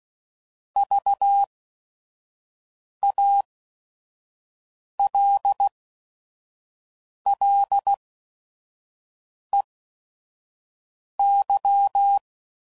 Morse code, heard as a keyed tone, about 16 wpm, spaced as for 5 wpm, 800 Hz. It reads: VALLEY
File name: CW Code Groups @5WPM / fromSM's / Cities